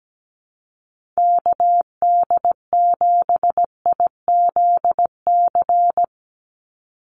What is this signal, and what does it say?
Morse code, heard as a keyed tone, about 17 wpm, 700 Hz. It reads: KD7IZC
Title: USA Call Signs 17wpm